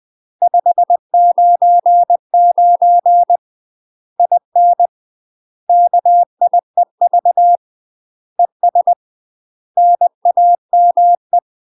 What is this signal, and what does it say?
Morse code, heard as a keyed tone, about 20 wpm, 700 Hz. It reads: 599 IN KIEV ES NAME
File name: UK MASTER QSOs TRAINING / 20WPM / UK MASTER QSOs 0038